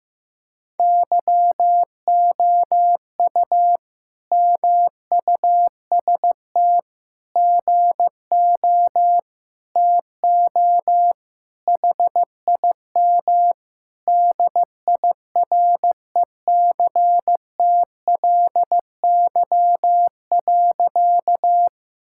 Morse code, heard as a keyed tone, about 14 wpm, 700 Hz. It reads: YOU MUST GO TO HIM DIRECTLY.
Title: Sentences from Top 1000 Words - Review Part 5 - 15wpm